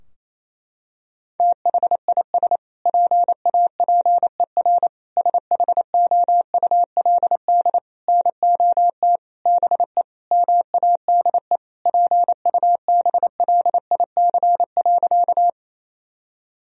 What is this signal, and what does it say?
Morse code, heard as a keyed tone, about 28 wpm, 700 Hz. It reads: THIS PAPER SHOULD NOT BE MADE PUBLIC.
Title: Sentences from Top 1000 Words - Review Part 3 - 28wpm